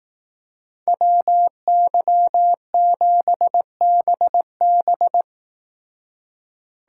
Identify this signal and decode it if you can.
Morse code, heard as a keyed tone, about 18 wpm, 700 Hz. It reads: WY7BB
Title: Speed Racing - USA 2x2 Call Signs 15wpm